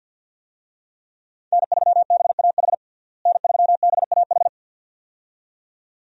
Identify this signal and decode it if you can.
Morse code, heard as a keyed tone, about 47 wpm, 700 Hz. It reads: N3BAH N3BAH
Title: USA Call Signs - 5 Characters - Repeated 2x - 50wpm